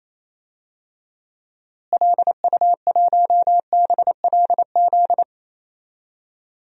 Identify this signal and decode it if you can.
Morse code, heard as a keyed tone, about 28 wpm, 700 Hz. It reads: LU1BLZ
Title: DXCC-100 Call Signs - 6 Characters - Repeated 3x - 28wpm